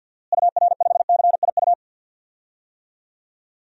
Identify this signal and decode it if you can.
Morse code, heard as a keyed tone, about 52 wpm, 700 Hz. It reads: UR5CIU
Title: DXCC-100 Call Signs - 6 Characters - Repeated 3x - 50wpm